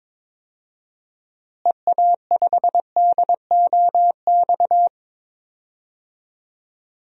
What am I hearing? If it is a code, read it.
Morse code, heard as a keyed tone, about 22 wpm, 700 Hz. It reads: EA5DOX
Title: DXCC-100 Call Signs - 6 Characters - Repeated 2x - 22wpm